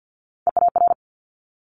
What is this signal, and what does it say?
Morse code, heard as a keyed tone, about 50 wpm, 700 Hz. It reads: EAR
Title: Multi-letter Phonemes Quick Fire 50wpm